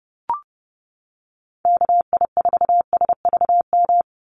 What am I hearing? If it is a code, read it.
Morse code, heard as a keyed tone, about 32 wpm, 700 Hz. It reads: KI4SVM
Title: NA SOTA Activator Callsigns 30wpm